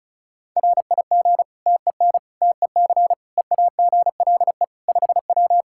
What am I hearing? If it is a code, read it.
Morse code, heard as a keyed tone, about 34 wpm, 700 Hz. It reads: RIG TEN TEC EAGLE 5W
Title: Practice QSO - Exchange 2 - Info - 35wpm